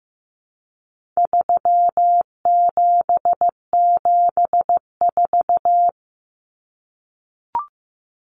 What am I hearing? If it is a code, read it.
Morse code, heard as a keyed tone, about 15 wpm, 700 Hz. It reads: 3774
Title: Common Cut Numbers - 4-Digit 15wpm